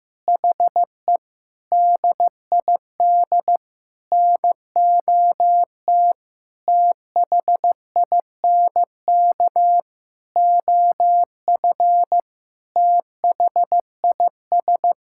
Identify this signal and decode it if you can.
Morse code, heard as a keyed tone, about 16 wpm, 700 Hz. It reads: HE DID NOT THINK OF THIS
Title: Sentences from Top 1600 Words 15wpm